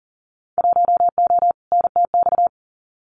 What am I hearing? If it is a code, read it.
Morse code, heard as a keyed tone, about 40 wpm, 700 Hz. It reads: 1O NTX
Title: Winter Field Day Exchanges 40wpm